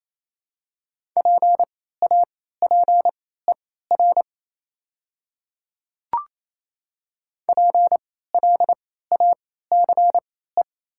Morse code, heard as a keyed tone, about 28 wpm, spaced as for 16 wpm, 700 Hz. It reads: PAPER PLACE
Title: 3x Character Spacing - Top 300 Words - 28wpm